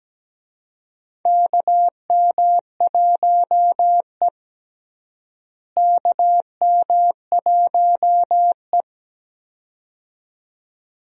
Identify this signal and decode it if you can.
Morse code, heard as a keyed tone, about 17 wpm, 700 Hz. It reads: KM1E KM1E
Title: USA Call Signs - 4 Characters - Repeated 2x - 17wpm